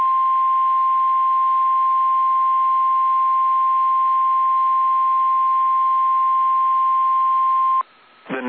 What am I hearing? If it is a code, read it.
Emergency Alert System (SAME) tones (the 1050 Hz attention tone). It is heard from 0.4 to 7.5 s.
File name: eas / KSDZKDJL / KSDZKDJL